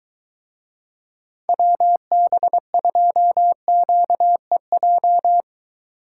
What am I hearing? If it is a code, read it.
Morse code, heard as a keyed tone, about 22 wpm, 700 Hz. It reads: WB2QEJ